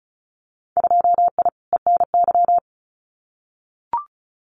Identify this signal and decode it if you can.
Morse code, heard as a keyed tone, about 36 wpm, 700 Hz. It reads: 2I ENY
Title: Winter Field Day Exchanges Encore 35wpm